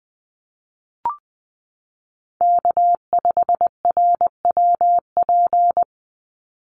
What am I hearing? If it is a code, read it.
Morse code, heard as a keyed tone, about 20 wpm, 690 Hz. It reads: K5RWP
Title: NA SOTA Chaser Callsigns 20wpm